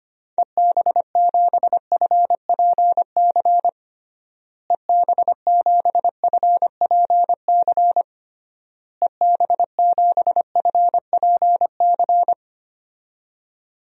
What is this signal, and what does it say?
Morse code, heard as a keyed tone, about 25 wpm, 700 Hz. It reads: EB7FPC EB7FPC EB7FPC
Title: DXCC-100 Call Signs - 6 Characters - Repeated 3x - 25wpm